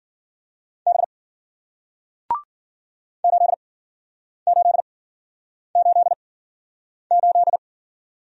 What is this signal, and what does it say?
Morse code, heard as a keyed tone, about 53 wpm, spaced as for 9 wpm, 700 Hz. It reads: B 8888